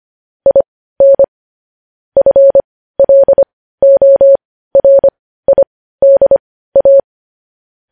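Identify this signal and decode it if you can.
Morse code, heard as a keyed tone, about 25 wpm, spaced as for 15 wpm, 550 Hz. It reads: IN FLORIDA
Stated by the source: Morse code practice of news headlines at 15 wpm, opening with vvv.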